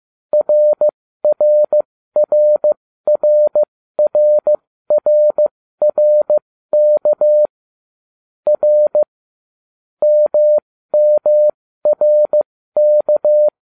Morse code, heard as a keyed tone, about 15 wpm, spaced as for 13 wpm, 600 Hz. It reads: RRRRRRRK R MMRK